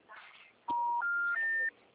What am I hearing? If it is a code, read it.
Special information tone.